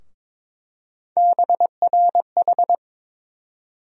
Morse code, heard as a keyed tone, about 22 wpm, 700 Hz. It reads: BRH
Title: USA Call Sign Postfixes - 3 Letters 22wpm